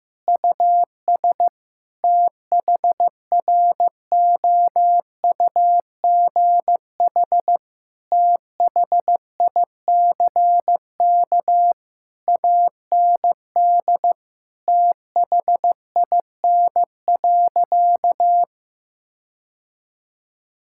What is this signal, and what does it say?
Morse code, heard as a keyed tone, about 15 wpm, 700 Hz. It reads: US THROUGH THICK AND THIN.
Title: Sentences from Top 1600 Words 15wpm